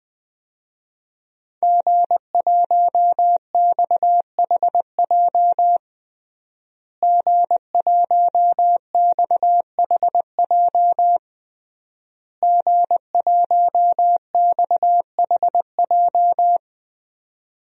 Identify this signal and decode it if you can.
Morse code, heard as a keyed tone, about 20 wpm, 700 Hz. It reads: G1XHJ G1XHJ G1XHJ
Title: DXCC-100 Call Signs - 5 Characters - Repeated 3x - 20wpm